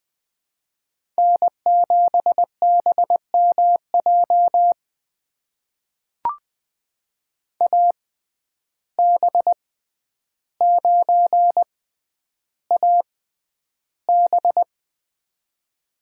Morse code, heard as a keyed tone, about 20 wpm, spaced as for 10 wpm, 700 Hz. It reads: N7BMJ AB9AB
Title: USA Call Signs - 5 Characters - 6x Character Spacing - 20wpm